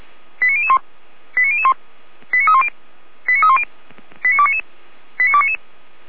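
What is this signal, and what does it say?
Five-tone selective call (ZVEI), heard as quick runs of five tones, tones of about 70 ms. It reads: 89001 89001 89310 89310 89299 89299
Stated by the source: ZVEI five-tone selective call received via SDR.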